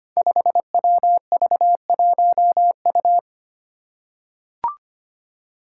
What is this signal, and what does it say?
Morse code, heard as a keyed tone, about 25 wpm, 700 Hz. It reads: 5WV1U
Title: Five Challenging Letter-Number Combinations 25wpm